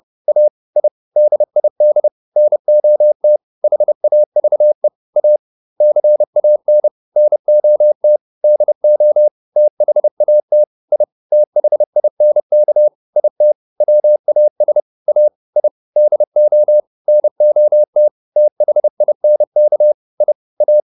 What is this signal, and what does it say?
Morse code, heard as a keyed tone, about 30 wpm, 600 Hz. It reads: A I DID NOT HAVE A CAN NOT DO THAT I THINK IT WAS A I DO NOT THINK I A